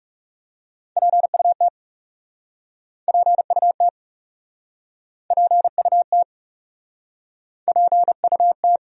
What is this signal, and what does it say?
Morse code, heard as a keyed tone, about 40 wpm, 700 Hz. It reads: PUT PUT PUT PUT